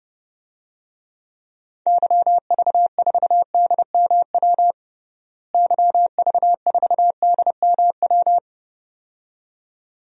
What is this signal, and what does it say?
Morse code, heard as a keyed tone, about 31 wpm, 700 Hz. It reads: YV4DMW YV4DMW
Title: DXCC-100 Call Signs - 6 Characters - Repeated 2x - 30wpm